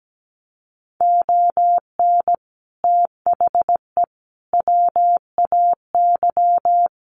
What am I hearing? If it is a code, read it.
Morse code, heard as a keyed tone, about 17 wpm, 700 Hz. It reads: ON THE WAY
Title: Sets of 5 Words 17wpm